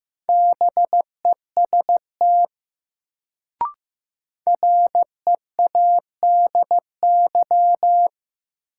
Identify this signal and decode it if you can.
Morse code, heard as a keyed tone, about 15 wpm, 700 Hz. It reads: BEST READY
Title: Sentences from Top 400 Words - New Word Drill - 15wpm